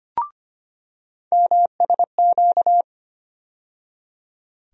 Morse code, heard as a keyed tone, about 25 wpm, 700 Hz. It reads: MSQ
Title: USA Call Sign Postfixes - 3 Letters 25wpm